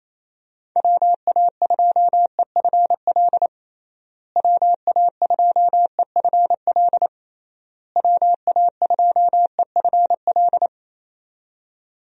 Morse code, heard as a keyed tone, about 28 wpm, 700 Hz. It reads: WA2EFL WA2EFL WA2EFL
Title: USA Call Signs - 6 Characters - Repeated 3x - 28wpm